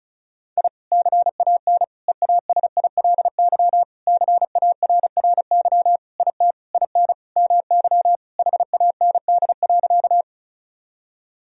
Morse code, heard as a keyed tone, about 36 wpm, 700 Hz. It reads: I CAN EASILY CARRY IT IN MY HAND.
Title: Sentences from Top 1000 Words - Review Part 1 - 35wpm